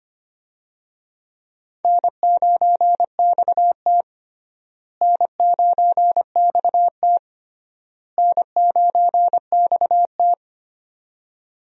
Morse code, heard as a keyed tone, about 25 wpm, 700 Hz. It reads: N9XT N9XT N9XT
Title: USA Call Signs - 4 Characters - Repeated 3x - 25wpm